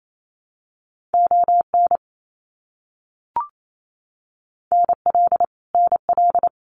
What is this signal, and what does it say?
Morse code, heard as a keyed tone, about 28 wpm, 700 Hz. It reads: ON NL NL
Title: Canadian Provinces and Territories - Repeated 3x - 28wpm